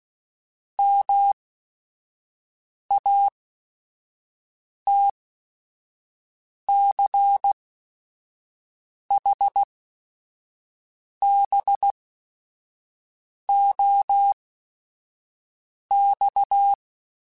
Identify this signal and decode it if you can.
Morse code, heard as a keyed tone, about 16 wpm, spaced as for 5 wpm, 800 Hz. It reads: MATCHBOX
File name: QSOs / QSOs-mp3s / WD0FBT 1@5